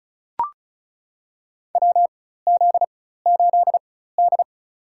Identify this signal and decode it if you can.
Morse code, heard as a keyed tone, about 36 wpm, spaced as for 16 wpm, 700 Hz. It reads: WZ8D
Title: USA Call Signs - 4 Characters - 4x Character Spacing - 35wpm